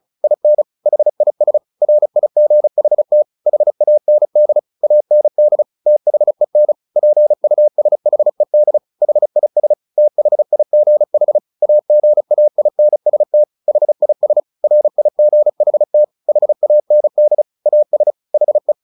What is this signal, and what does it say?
Morse code, heard as a keyed tone, about 34 wpm, 600 Hz. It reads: IN HIS RIGHT HAND AND THEN PUSHED HIS THIGH AGAINST HIS RIGHT HAND AS HE
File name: the old man and the sea 2 35wpm